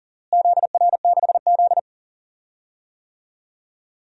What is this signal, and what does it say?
Morse code, heard as a keyed tone, about 40 wpm, 690 Hz. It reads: ZR6Z
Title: DXCC-100 Call Signs - 4 Characters - Repeated 2x - 40wpm